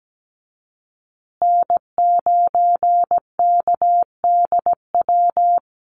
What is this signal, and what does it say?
Morse code, heard as a keyed tone, about 17 wpm, 700 Hz. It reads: N9KDW
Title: USA Call Signs Repeated 3x 17wpm